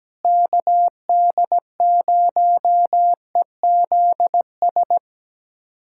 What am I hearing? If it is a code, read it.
Morse code, heard as a keyed tone, about 17 wpm, 700 Hz. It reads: KD0EZS
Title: USA Callsign DE Callsign 17wpm